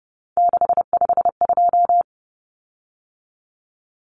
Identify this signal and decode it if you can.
Morse code, heard as a keyed tone, about 30 wpm, 700 Hz. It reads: 652
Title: Common Cut Numbers - 3-Digit 30wpm